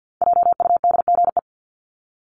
Morse code, heard as a keyed tone, about 48 wpm, 700 Hz. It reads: JUDGE
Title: Top 1751-2000 Words 50wpm